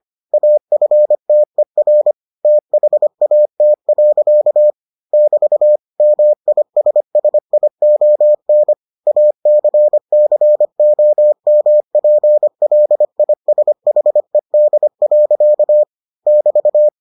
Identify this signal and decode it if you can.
Morse code, heard as a keyed tone, about 25 wpm, 600 Hz. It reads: AFTER THAT. <BT> MISSION ACCOMPLISHED. <BT>